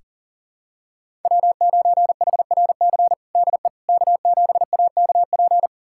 Morse code, heard as a keyed tone, about 40 wpm, 700 Hz. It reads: W9HRC DE K7AKP
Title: USA Callsign DE Callsign - 5 Characters - 40wpm